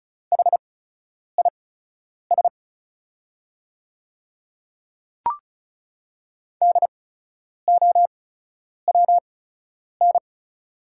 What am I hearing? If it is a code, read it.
Morse code, heard as a keyed tone, about 36 wpm, spaced as for 10 wpm, 700 Hz. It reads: HIS DOWN
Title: 8x Character Spacing - Top 100 Words - 35wpm